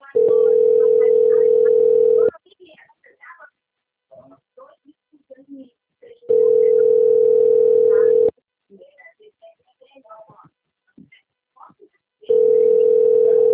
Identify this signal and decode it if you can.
Ringback tone (North America), 440 + 480 Hz, with speech.
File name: out- - - - - .21348